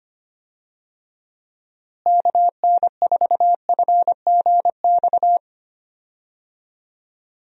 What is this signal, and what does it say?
Morse code, heard as a keyed tone, about 25 wpm, 700 Hz. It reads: KN4FGX
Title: USA Call Signs - 6 Characters - Repeated 2x - 25wpm